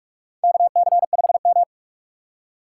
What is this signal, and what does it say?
Morse code, heard as a keyed tone, about 46 wpm, 700 Hz. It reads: KC5M
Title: USA Call Signs - 4 Characters - 2x Character Spacing - 45wpm